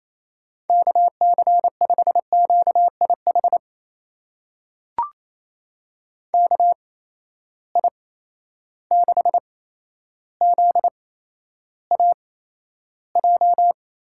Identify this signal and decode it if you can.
Morse code, heard as a keyed tone, about 28 wpm, spaced as for 12 wpm, 700 Hz. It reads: KC5QIH KI6ZAJ
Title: USA Call Signs - 6 Characters - 8x Character Spacing - 28wpm